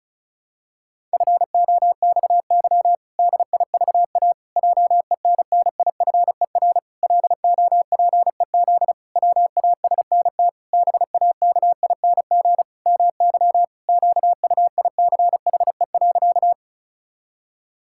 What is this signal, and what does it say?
Morse code, heard as a keyed tone, about 36 wpm, 700 Hz. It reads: FOXY DIVA JENNIFER LOPEZ WASNT BAKING MY QUICHE.
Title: Pangram Challenge v2 - 35wpm